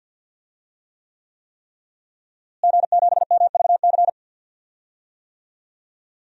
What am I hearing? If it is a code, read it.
Morse code, heard as a keyed tone, about 51 wpm, 700 Hz. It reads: G7MVC
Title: DXCC-100 Call Signs - 5 Characters - Repeated 3x - 50wpm